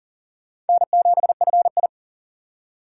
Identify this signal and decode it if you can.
Morse code, heard as a keyed tone, about 52 wpm, 690 Hz. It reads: N7FI